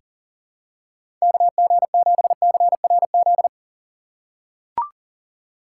Morse code, heard as a keyed tone, about 40 wpm, 700 Hz. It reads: KG7CRZ
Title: USA Call Signs - 6 Characters - 4x Character Spacing - 40wpm